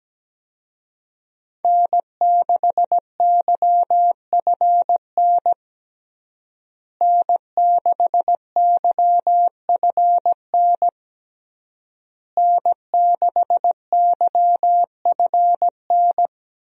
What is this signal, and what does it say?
Morse code, heard as a keyed tone, about 17 wpm, 700 Hz. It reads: N6YFN N6YFN N6YFN
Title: USA Call Signs - 5 Characters - Repeated 3x - 17wpm